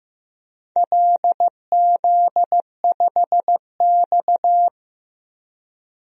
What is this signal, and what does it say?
Morse code, heard as a keyed tone, about 15 wpm, 700 Hz. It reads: LZ5X
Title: DXCC-100 Call Signs - 4 Characters - 15wpm